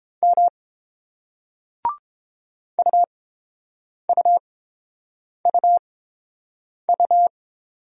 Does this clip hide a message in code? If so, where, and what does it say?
Yes: Morse code from 0.2 to 7.3 s, heard as a keyed tone, about 31 wpm, spaced as for 7 wpm, 700 Hz. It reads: M UUUU